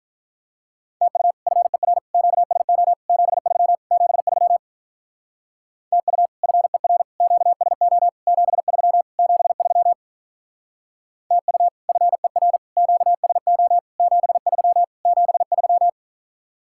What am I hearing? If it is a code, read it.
Morse code, heard as a keyed tone, about 44 wpm, 700 Hz. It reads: TU FER QSO 73 73 TU FER QSO 73 73 TU FER QSO 73 73